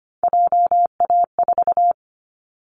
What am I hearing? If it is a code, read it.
Morse code, heard as a keyed tone, about 25 wpm, 700 Hz. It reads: JA4
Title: Callsign Prefixes for Top 25 DXCC - Repeated 3x - 25wpm